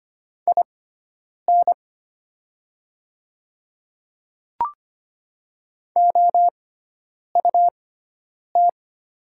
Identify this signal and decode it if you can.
Morse code, heard as a keyed tone, about 25 wpm, spaced as for 9 wpm, 700 Hz. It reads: IN OUT
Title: 6x Character Spacing - Top 100 Words - 25wpm